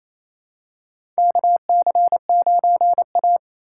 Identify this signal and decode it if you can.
Morse code, heard as a keyed tone, about 27 wpm, 700 Hz. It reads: KC9A
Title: USA Call Signs - 4 Characters - 4x Character Spacing - 28wpm